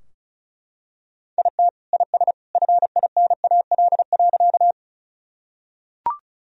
Morse code, heard as a keyed tone, about 34 wpm, 700 Hz. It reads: IT IS FINAL.
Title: Sentences from Top 1000 Words - Review Part 2 - 35wpm